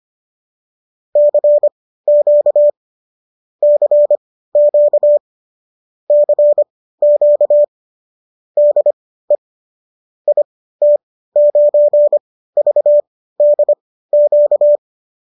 Morse code, heard as a keyed tone, about 25 wpm, spaced as for 15 wpm, 590 Hz. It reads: CQ CQ CQ DE IT9VDQ